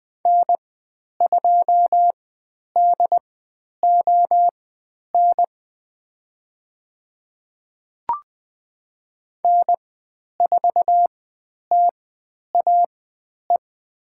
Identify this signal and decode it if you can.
Morse code, heard as a keyed tone, about 20 wpm, spaced as for 10 wpm, 700 Hz. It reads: N2DON N4TAE
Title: 095 Lesson 16g - Letter c - Call Signs 20wpm